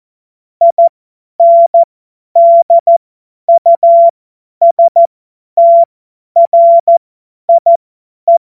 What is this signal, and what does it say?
Morse code, heard as a keyed tone, about 14 wpm, spaced as for 10 wpm, 680 Hz. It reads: INDUSTRIE